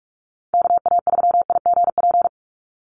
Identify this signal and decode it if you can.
Morse code, heard as a keyed tone, about 44 wpm, 700 Hz. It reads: KA3IGP
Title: USA Call Signs 45wpm